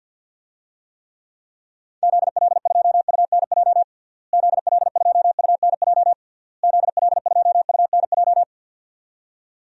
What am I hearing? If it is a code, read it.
Morse code, heard as a keyed tone, about 52 wpm, 700 Hz. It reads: ZL2UNJ ZL2UNJ ZL2UNJ
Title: DXCC-100 Call Signs - 6 Characters - Repeated 3x - 50wpm